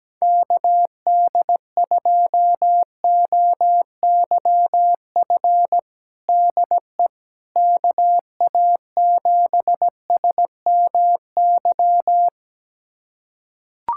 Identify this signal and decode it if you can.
Morse code, heard as a keyed tone, about 17 wpm, 700 Hz. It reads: KD2OYF DE KA7SMY
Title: USA Callsign DE Callsign - 6 Characters - 17wpm